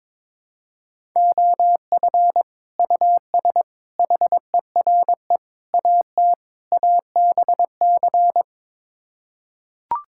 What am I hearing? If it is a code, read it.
Morse code, heard as a keyed tone, about 21 wpm, 700 Hz. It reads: OF US HERE AT ABC
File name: practice / 22wpm / Build A Phrase - 5-Word Phrases - 22wpm